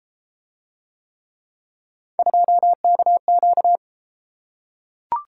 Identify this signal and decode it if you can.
Morse code, heard as a keyed tone, about 30 wpm, 700 Hz. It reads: 2KQ